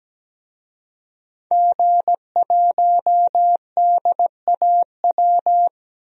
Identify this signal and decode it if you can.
Morse code, heard as a keyed tone, about 17 wpm, 700 Hz. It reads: G1DAW
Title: DXCC-100 Call Signs - 5 Characters - 17wpm